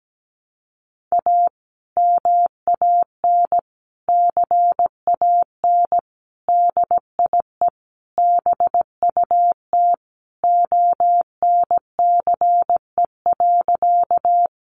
Morse code, heard as a keyed tone, about 17 wpm, 700 Hz. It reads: A MAN CAN DIE BUT ONCE.
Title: Sentences from Top 1300 Words 17wpm